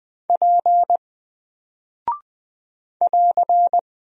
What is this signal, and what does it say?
Morse code, heard as a keyed tone, about 20 wpm, 700 Hz. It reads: P <AR>
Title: Single Character (Letter:Number:Punctuation:Prosign) Alt 20wpm